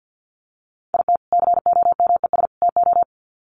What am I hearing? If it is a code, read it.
Morse code, heard as a keyed tone, about 50 wpm, 700 Hz. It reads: IT COMES TO
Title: Sets of 3 Words 50wpm